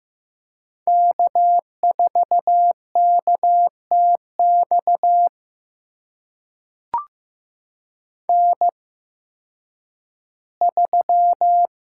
Morse code, heard as a keyed tone, about 15 wpm, spaced as for 10 wpm, 700 Hz. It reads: K4KTX N3
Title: USA Call Signs - 5 Characters - 8x Character Spacing - 15wpm